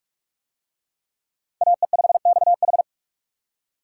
Morse code, heard as a keyed tone, about 40 wpm, 700 Hz. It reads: AE5XH